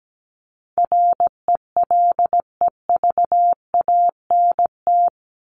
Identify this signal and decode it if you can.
Morse code, heard as a keyed tone, about 17 wpm, 700 Hz. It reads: RELEVANT
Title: Top 1001-1250 Words 17wpm